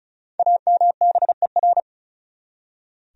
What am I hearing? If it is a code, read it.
Morse code, heard as a keyed tone, about 36 wpm, 700 Hz. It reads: AMBER
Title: USA Common Names - 35wpm